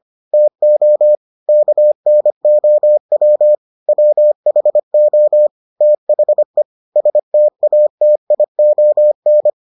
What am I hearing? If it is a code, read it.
Morse code, heard as a keyed tone, about 25 wpm, 600 Hz. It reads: TO KNOW WHO THE STATION